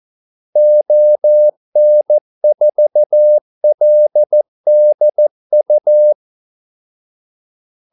Morse code, heard as a keyed tone, about 14 wpm, 600 Hz. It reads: ON4LDU
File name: Callsigns-14wpm